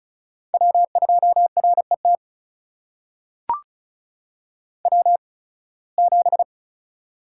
Morse code, heard as a keyed tone, about 36 wpm, spaced as for 23 wpm, 700 Hz. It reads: W2RET W7
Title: USA Call Signs - 5 Characters - 8x Character Spacing - 35wpm